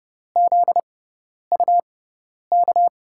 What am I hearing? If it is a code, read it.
Morse code, heard as a keyed tone, about 31 wpm, spaced as for 10 wpm, 700 Hz. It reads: ZUK
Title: USA Call Signs - 6 Characters - 6x Character Spacing - 30wpm